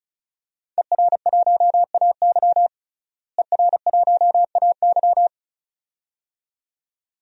Morse code, heard as a keyed tone, about 34 wpm, 700 Hz. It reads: ER1AY ER1AY
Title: DXCC-100 Call Signs - 5 Characters - Repeated 2x - 35wpm